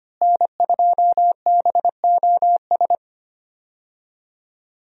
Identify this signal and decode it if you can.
Morse code, heard as a keyed tone, about 25 wpm, 700 Hz. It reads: N2BOS
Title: USA Call Signs - 5 Characters - Repeated 3x - 25wpm